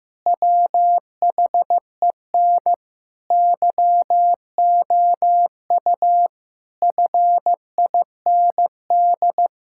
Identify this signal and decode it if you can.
Morse code, heard as a keyed tone, about 14 wpm, 700 Hz. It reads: WHEN YOU FIND
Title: Sentences from Top 1000 Words - Review Part 5 - 15wpm